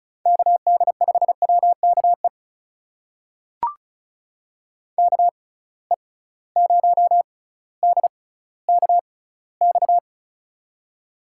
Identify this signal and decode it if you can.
Morse code, heard as a keyed tone, about 36 wpm, spaced as for 18 wpm, 700 Hz. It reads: KD5WKE KE0DKX
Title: USA Call Signs - 6 Characters - 6x Character Spacing - 35wpm